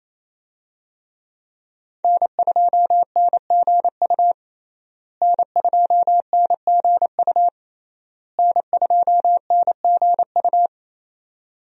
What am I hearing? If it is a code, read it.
Morse code, heard as a keyed tone, about 28 wpm, 700 Hz. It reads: N2NGU N2NGU N2NGU
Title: USA Call Signs - 5 Characters - Repeated 3x - 28wpm